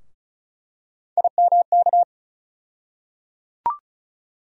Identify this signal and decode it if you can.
Morse code, heard as a keyed tone, about 34 wpm, 700 Hz. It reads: IMK
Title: USA Call Sign Postfixes - 3 Letters 35wpm